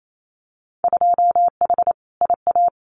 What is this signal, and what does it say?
Morse code, heard as a keyed tone, about 28 wpm, 700 Hz. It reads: 2H IA
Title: Winter Field Day Exchanges Encore 28wpm